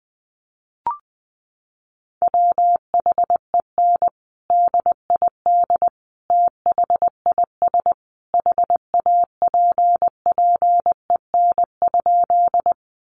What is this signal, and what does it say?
Morse code, heard as a keyed tone, about 20 wpm, 700 Hz. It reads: WHEN DID THIS HAPPEN?
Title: Sentences from Top 400 Words 20wpm